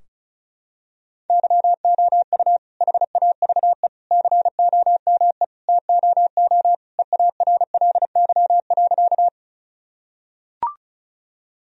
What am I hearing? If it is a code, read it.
Morse code, heard as a keyed tone, about 36 wpm, 700 Hz. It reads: YOU HAVE COME TOO EARLY.
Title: Sentences from Top 1000 Words - Review Part 2 - 35wpm